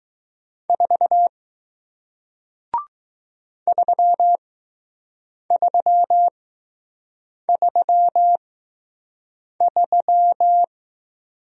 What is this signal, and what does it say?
Morse code, heard as a keyed tone, about 20 wpm, spaced as for 6 wpm, 700 Hz. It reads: 4 3333